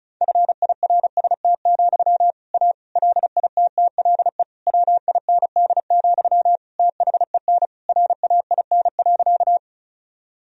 Morse code, heard as a keyed tone, about 36 wpm, 700 Hz. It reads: FIRST, A LITTLE WIND, THEN RAIN.
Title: Sentences from Top 1000 Words - Review Part 2 - 35wpm